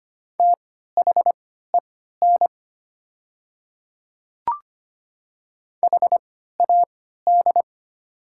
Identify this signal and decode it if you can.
Morse code, heard as a keyed tone, about 25 wpm, spaced as for 14 wpm, 700 Hz. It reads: THEN HAD
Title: 3x Character Spacing - Top 100 Words - 25wpm